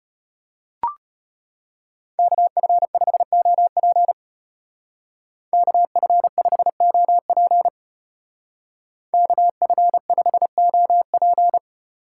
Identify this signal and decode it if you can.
Morse code, heard as a keyed tone, about 31 wpm, 700 Hz. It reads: KF5OP KF5OP KF5OP
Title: Speed Racing - USA 2x2 Call Signs 25wpm